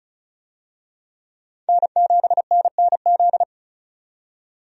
Morse code, heard as a keyed tone, about 34 wpm, 700 Hz. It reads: N7NNZ
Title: USA Call Signs - 5 Characters - 35wpm